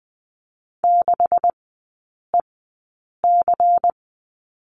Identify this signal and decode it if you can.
Morse code, heard as a keyed tone, about 20 wpm, spaced as for 8 wpm, 700 Hz. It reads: 6EC
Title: 127 Lesson 21e - Number 6 - Three Characters 20wpm